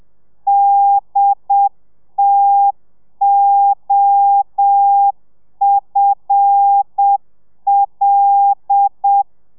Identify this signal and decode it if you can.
Morse code, heard as a keyed tone, about 7 wpm, 800 Hz. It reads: DTOFL
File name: telegrafia / i2tfj / Lez5 30